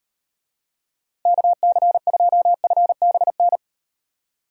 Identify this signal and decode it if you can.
Morse code, heard as a keyed tone, about 38 wpm, 700 Hz. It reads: KC2FBN